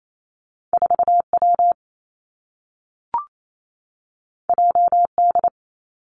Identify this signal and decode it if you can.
Morse code, heard as a keyed tone, about 28 wpm, 700 Hz. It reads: 4 W J D
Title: Two Challenging Letter-Number Combinations 28wpm